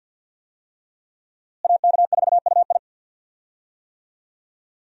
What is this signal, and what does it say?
Morse code, heard as a keyed tone, about 50 wpm, 700 Hz. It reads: AK4UI
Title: USA Call Signs - 5 Characters - Repeated 3x - 50wpm